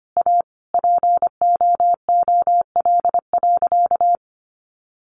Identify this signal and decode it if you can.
Morse code, heard as a keyed tone, about 25 wpm, 700 Hz. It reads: A POOL.
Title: Easy Phrases 25wpm